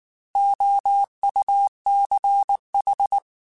Morse code, heard as a keyed tone, about 19 wpm, 800 Hz. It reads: OUCH
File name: Alphabetical / Alphabetical mp3s / Os @18